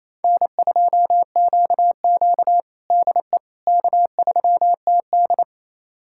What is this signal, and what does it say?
Morse code, heard as a keyed tone, about 28 wpm, 700 Hz. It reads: N2QQ DE K3TD
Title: USA Callsign DE Callsign - 4 Characters - 28wpm